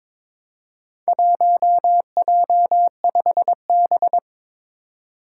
Morse code, heard as a keyed tone, about 22 wpm, 700 Hz. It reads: 1J5B
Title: Four Challenging Letter-Number Combinations 22wpm